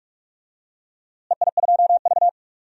Morse code, heard as a keyed tone, about 46 wpm, 700 Hz. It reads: EI2V
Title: DXCC-100 Call Signs - 4 Characters - Repeated 3x - 45wpm